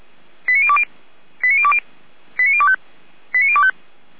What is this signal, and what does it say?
Five-tone selective call (ZVEI), heard as quick runs of five tones, tones of about 70 ms. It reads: 89020 89020 89025 89025
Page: SDR/ZVEI